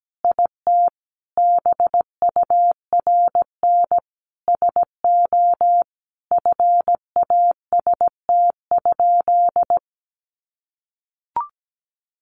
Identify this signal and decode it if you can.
Morse code, heard as a keyed tone, about 17 wpm, 700 Hz. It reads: IT BURN SO FAST?
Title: Sentences from Top 1300 Words 17wpm